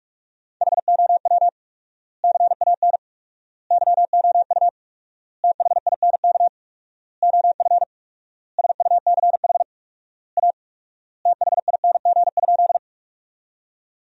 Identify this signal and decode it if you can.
Morse code, heard as a keyed tone, about 47 wpm, 700 Hz. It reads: HOW CAN YOU THINK OF SUCH A THING?
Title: 4x Word-Spacing - Sentences from Top 200 Words - 45wpm